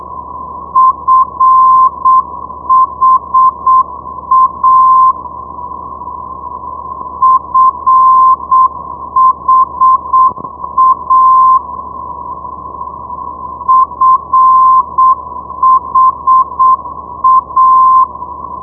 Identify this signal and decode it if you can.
Morse code, heard as a keyed tone, about 7 wpm, 1040 Hz. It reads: FHA FHA FHA
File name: ndb friedrichshafen fha 473.0